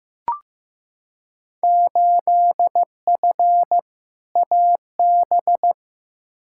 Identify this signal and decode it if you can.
Morse code, heard as a keyed tone, about 15 wpm, 700 Hz. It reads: 8F AB
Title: Field Day Exchanges Encore - 15wpm